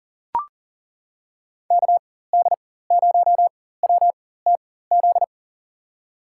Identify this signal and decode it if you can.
Morse code, heard as a keyed tone, about 40 wpm, spaced as for 19 wpm, 700 Hz. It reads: KD0WTZ
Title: USA Call Signs - 6 Characters - 4x Character Spacing - 40wpm